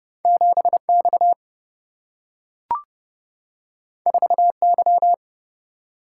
Morse code, heard as a keyed tone, about 30 wpm, 700 Hz. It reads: 7X 4Y
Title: Single-Letter Single-Number 30wpm